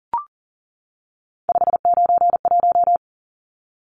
Morse code, heard as a keyed tone, about 40 wpm, 700 Hz. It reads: 591
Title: Numbers - 3-Digit 40wpm